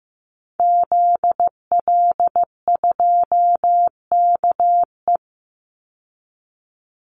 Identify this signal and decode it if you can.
Morse code, heard as a keyed tone, about 15 wpm, 700 Hz. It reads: ZL2KE
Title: Call Signs Repeated 3x - 15wpm